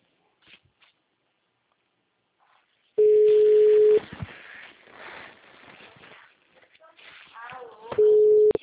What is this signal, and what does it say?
Ringback tone (Europe, 425 Hz), with speech.
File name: out- -2010- - - .7745